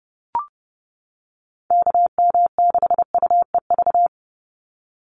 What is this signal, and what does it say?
Morse code, heard as a keyed tone, about 30 wpm, 700 Hz. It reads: KM6UEV
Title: USA Call Signs 30wpm